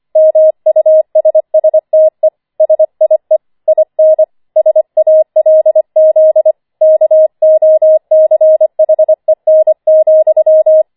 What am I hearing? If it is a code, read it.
Morse code, heard as a keyed tone, about 25 wpm, 630 Hz. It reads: MUSSTE SIE IN SALZ KOCHEN,
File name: 25wpm